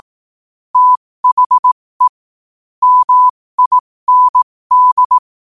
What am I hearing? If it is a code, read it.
Morse code, heard as a keyed tone, about 18 wpm, spaced as for 15 wpm, 1000 Hz. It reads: THE MIND